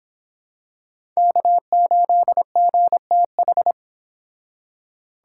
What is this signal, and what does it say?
Morse code, heard as a keyed tone, about 26 wpm, 700 Hz. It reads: K8GTH